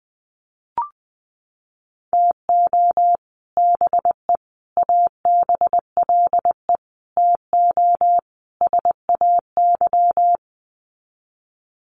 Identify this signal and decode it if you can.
Morse code, heard as a keyed tone, about 20 wpm, 700 Hz. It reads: TO BE ABLE TO SAY
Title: Sets of 5 Words Encore2 20wpm